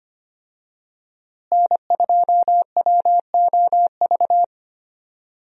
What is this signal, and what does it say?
Morse code, heard as a keyed tone, about 25 wpm, 700 Hz. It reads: N2WOV
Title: USA Call Signs - 5 Characters - Repeated 2x - 25wpm